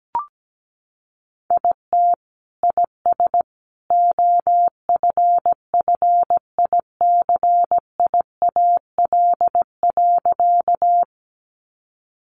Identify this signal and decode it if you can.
Morse code, heard as a keyed tone, about 17 wpm, 700 Hz. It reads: IT IS OFFICIAL.
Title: Sentences from Top 1000 Words 17wpm